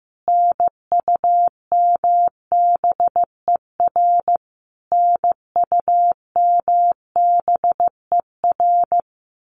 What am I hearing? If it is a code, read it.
Morse code, heard as a keyed tone, about 14 wpm, 700 Hz. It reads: NUMBER NUMBER
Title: Top 300 Words Repeated 15wpm